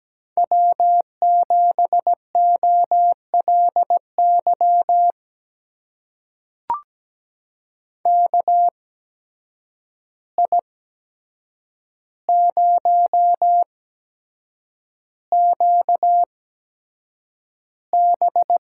Morse code, heard as a keyed tone, about 17 wpm, spaced as for 7 wpm, 700 Hz. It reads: W7OLY KI0QB
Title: USA Call Signs - 8x Character Spacing - 17wpm